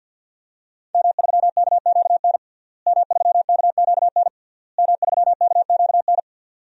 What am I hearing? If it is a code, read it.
Morse code, heard as a keyed tone, about 51 wpm, 700 Hz. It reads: M3KQN M3KQN M3KQN
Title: DXCC-100 Call Signs - 5 Characters - Repeated 3x - 50wpm